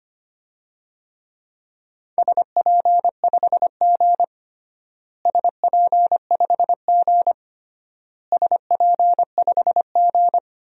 Morse code, heard as a keyed tone, about 25 wpm, 700 Hz. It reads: SP5G SP5G SP5G
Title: DXCC-100 Call Signs - 4 Characters - Repeated 3x - 25wpm